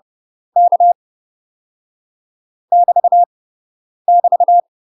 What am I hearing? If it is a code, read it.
Morse code, heard as a keyed tone, about 30 wpm, 700 Hz. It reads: K <BT> <BT>
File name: UK MASTER QSOs TRAINING / 30WPM / UK MASTER QSOs 0048